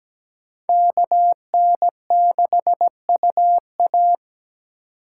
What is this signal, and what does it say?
Morse code, heard as a keyed tone, about 20 wpm, 700 Hz. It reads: KN6UA
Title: Speed Racing - USA Call Signs 17wpm